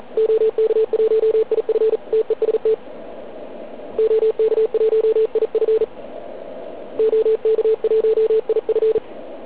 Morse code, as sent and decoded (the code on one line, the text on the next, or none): --- -.- .---- .. ..-. / - . ... - / --- -.- .---- .. ..-. / --- -.- .---- .. ..-.
OK1IF TEST OK1IF OK1IF